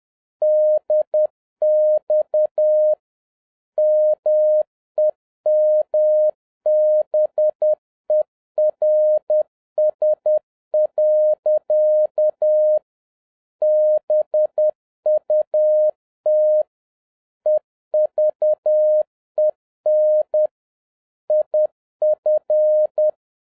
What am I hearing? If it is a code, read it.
Morse code, heard as a keyed tone, about 10 wpm, 610 Hz. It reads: DX MEMBERS. BUT EVEN IF